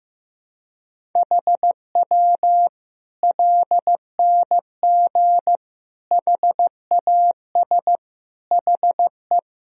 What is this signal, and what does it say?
Morse code, heard as a keyed tone, about 15 wpm, 700 Hz. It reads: HW LNG HAS HE
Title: Sentences from Top 100 Words and Abbreviations 15wpm